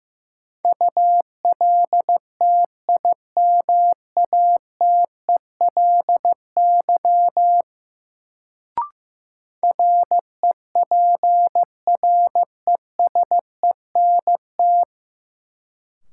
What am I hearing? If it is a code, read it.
Morse code, heard as a keyed tone, about 15 wpm, 700 Hz. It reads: ULTIMATELY REPRESENT
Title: Sentences from Top 1200 Words - New Word Drill Alt - 15wpm